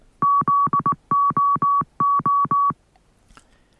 Morse code, heard as a keyed tone, about 19 wpm, 1130 Hz. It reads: ZOO